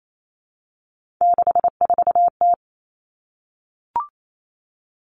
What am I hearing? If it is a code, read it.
Morse code, heard as a keyed tone, about 28 wpm, 700 Hz. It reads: 64T
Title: Common Cut Numbers - 3-Digit 28wpm